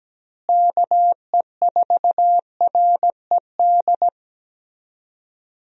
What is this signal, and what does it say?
Morse code, heard as a keyed tone, about 17 wpm, 700 Hz. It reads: KE4RED
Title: USA Call Signs - 6 Characters - Repeated 3x - 17wpm